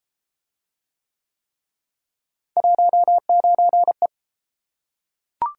Morse code, heard as a keyed tone, about 30 wpm, 700 Hz. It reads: 19E